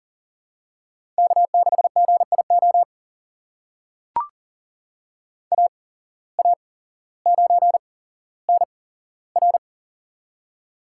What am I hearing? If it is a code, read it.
Morse code, heard as a keyed tone, about 40 wpm, spaced as for 17 wpm, 700 Hz. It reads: K6GIO AA9NR
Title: USA Call Signs - 5 Characters - 8x Character Spacing - 40wpm